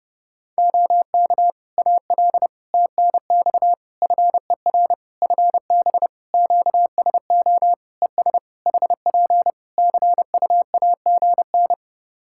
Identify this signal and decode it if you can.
Morse code, heard as a keyed tone, about 30 wpm, 700 Hz. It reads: OK AL TNX FER FB QSO ES HP CUAGN
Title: Practice QSO - Exchange 4 - Ending - 30wpm